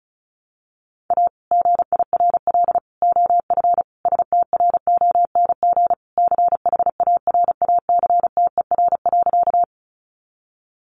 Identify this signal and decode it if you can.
Morse code, heard as a keyed tone, about 34 wpm, 700 Hz. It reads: A GIRL OF STRONG CHARACTER.
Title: Sentences from Top 1200 Words 35wpm